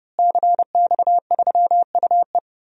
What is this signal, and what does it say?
Morse code, heard as a keyed tone, about 30 wpm, 700 Hz. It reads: CX3UE
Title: DXCC-100 Call Signs - 5 Characters - Repeated 3x - 30wpm